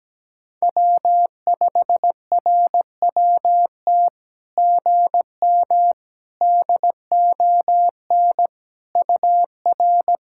Morse code, heard as a keyed tone, about 18 wpm, 700 Hz. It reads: W5RWT GM DON UR
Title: SOTA QSO Practice 17wpm